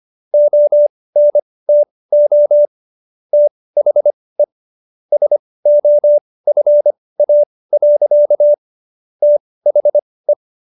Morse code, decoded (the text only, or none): ONTO THE SOFA. THE